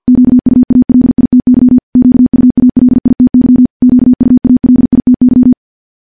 Morse code, heard as a keyed tone, about 48 wpm, 250 Hz. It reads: QUALITY QUALITY QUALITY
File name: morse-training-50WPM